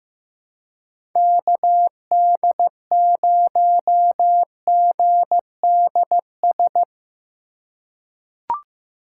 Morse code, heard as a keyed tone, about 15 wpm, 700 Hz. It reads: KD0GDS
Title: USA Call Signs - 6 Characters - 2x Character Spacing - 15wpm